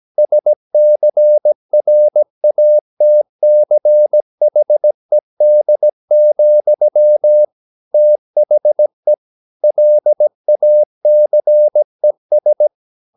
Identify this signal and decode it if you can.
Morse code, heard as a keyed tone, about 17 wpm, 600 Hz. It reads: SCRATCHED, THE LACES
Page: Romanen Never af Ken Follet på engelsk.